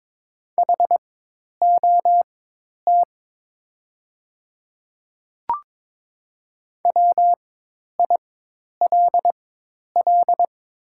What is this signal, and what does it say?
Morse code, heard as a keyed tone, about 22 wpm, spaced as for 10 wpm, 700 Hz. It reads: HOT WILL
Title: 4x Character Spacing - Top 100 Words - 22wpm